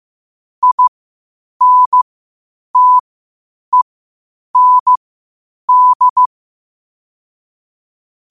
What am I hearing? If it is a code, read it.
Morse code, heard as a keyed tone, about 15 wpm, spaced as for 8 wpm, 1000 Hz. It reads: INTEND
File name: qotdfiles / 8wpm